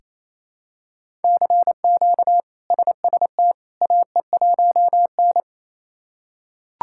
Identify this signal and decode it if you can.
Morse code, heard as a keyed tone, about 28 wpm, 700 Hz. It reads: CQ SST AE1N
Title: K1USN SST QSO Practice - 28wpm